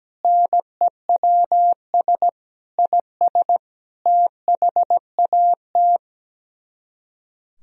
Morse code, heard as a keyed tone, about 17 wpm, 700 Hz. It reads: NEWS IS THAT
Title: Speed-Racing - Sets of 5 Words 17wpm